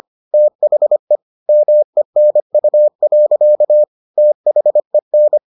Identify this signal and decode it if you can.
Morse code, heard as a keyed tone, about 25 wpm, 600 Hz. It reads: THE MENU. THEN